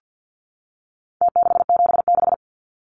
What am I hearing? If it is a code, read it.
Morse code, heard as a keyed tone, about 50 wpm, 700 Hz. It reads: T676
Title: Numbers - Contest Style - 4-Digit 50wpm